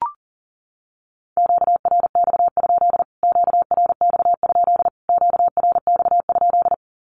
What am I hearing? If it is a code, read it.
Morse code, heard as a keyed tone, about 40 wpm, 700 Hz. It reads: QRX? QRX? QRX?
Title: Q-Codes Repeated 3x 40wpm